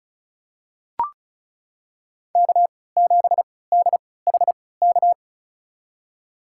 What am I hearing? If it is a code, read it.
Morse code, heard as a keyed tone, about 36 wpm, spaced as for 20 wpm, 700 Hz. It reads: K7DHK
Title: USA Call Signs - 5 Characters - 3x Character Spacing - 35wpm